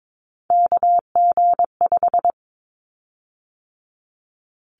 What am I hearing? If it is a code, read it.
Morse code, heard as a keyed tone, about 22 wpm, 700 Hz. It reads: KG5
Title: Callsign Prefixes for Top 25 DXCC - 22wpm